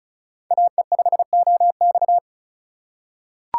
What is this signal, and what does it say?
Morse code, heard as a keyed tone, about 36 wpm, 700 Hz. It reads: AE5OX
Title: USA Call Signs - 5 Characters - 4x Character Spacing - 35wpm